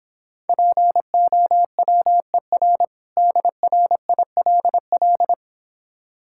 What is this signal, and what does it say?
Morse code, heard as a keyed tone, about 26 wpm, 700 Hz. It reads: POWER DRILL